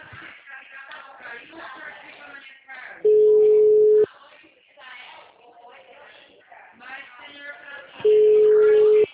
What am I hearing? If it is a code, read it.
Ringback tone (Europe, 425 Hz), with speech.